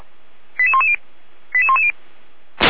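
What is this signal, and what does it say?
Five-tone selective call (ZVEI), heard as quick runs of five tones, tones of about 70 ms. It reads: 88110 88110
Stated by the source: ZVEI five-tone call 88192 received by SDR, Feuerwehr Orlamünde.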